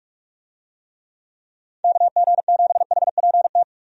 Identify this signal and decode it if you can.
Morse code, heard as a keyed tone, about 47 wpm, 700 Hz. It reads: KG7HPT
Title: USA Call Signs - 6 Characters - 45wpm